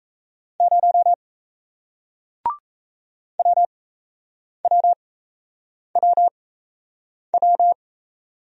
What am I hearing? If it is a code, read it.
Morse code, heard as a keyed tone, about 37 wpm, spaced as for 8 wpm, 700 Hz. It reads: 0 WWWW